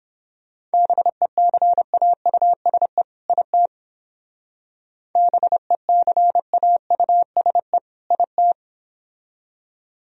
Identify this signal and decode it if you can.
Morse code, heard as a keyed tone, about 29 wpm, 700 Hz. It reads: BECAUSE IT BECAUSE IT